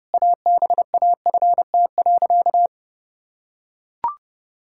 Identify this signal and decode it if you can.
Morse code, heard as a keyed tone, about 32 wpm, 700 Hz. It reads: ABAFT.
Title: Pangram Challenge v2 - 30wpm